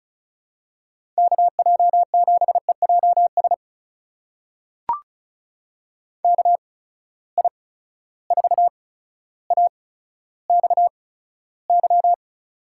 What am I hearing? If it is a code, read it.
Morse code, heard as a keyed tone, about 36 wpm, spaced as for 15 wpm, 700 Hz. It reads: KJ7EJS KI4AXY
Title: USA Call Signs - 8x Character Spacing - 35wpm